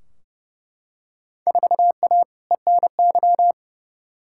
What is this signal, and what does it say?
Morse code, heard as a keyed tone, about 30 wpm, 700 Hz. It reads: 4A ENY
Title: Field Day Exchanges - 30wpm